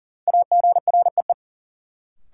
Morse code, heard as a keyed tone, about 38 wpm, 700 Hz. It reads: AGREE
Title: Sentences from Top 1000 Words - Review Part 6 - New Word Drill - 40wpm